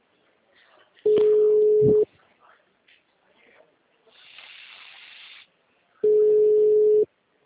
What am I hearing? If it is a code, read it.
Ringback tone (Europe, 425 Hz).
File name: out- -2011- - - .8